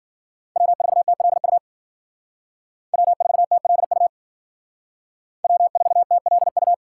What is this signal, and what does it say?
Morse code, heard as a keyed tone, about 52 wpm, 700 Hz. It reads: W4TLU W4TLU W4TLU